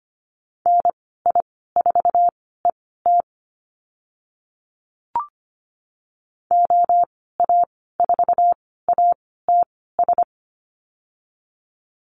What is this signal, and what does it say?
Morse code, heard as a keyed tone, about 25 wpm, spaced as for 16 wpm, 700 Hz. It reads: NI4ET OA4ATS
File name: Lesson 15g - Number 5 - Call Signs 25wpm